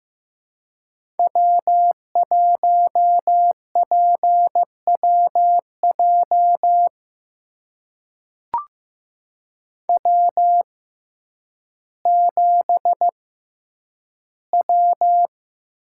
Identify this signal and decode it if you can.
Morse code, heard as a keyed tone, about 15 wpm, spaced as for 9 wpm, 700 Hz. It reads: W1PWJ W7W
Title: USA Call Signs - 5 Characters - 6x Character Spacing - 15wpm